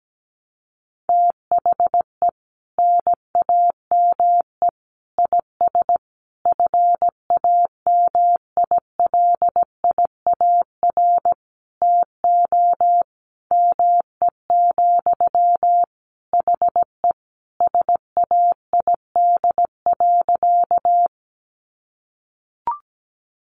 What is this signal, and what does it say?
Morse code, heard as a keyed tone, about 17 wpm, 700 Hz. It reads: THE NAME IS FAMILIAR TO ME, HE SAID.
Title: Sentences from Top 1000 Words 17wpm